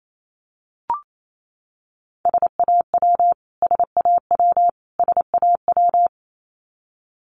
Morse code, heard as a keyed tone, about 28 wpm, 700 Hz. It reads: SAW SAW SAW
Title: Top 500 Words Repeated 28wpm